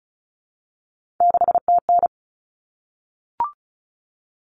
Morse code, heard as a keyed tone, about 34 wpm, 700 Hz. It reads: 6TN
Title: Common Cut Numbers - 3-Digit 35wpm